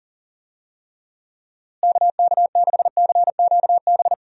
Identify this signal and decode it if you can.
Morse code, heard as a keyed tone, about 40 wpm, 700 Hz. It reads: KK6CQB
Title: USA Call Signs - 6 Characters - Repeated 2x - 40wpm